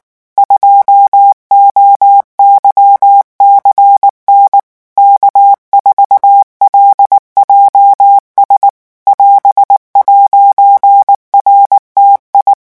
Morse code, heard as a keyed tone, about 19 wpm, 800 Hz. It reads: 2OYCN K4LJS <AS>'RTI
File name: morse mp3 / revision19-vitesse 18 mots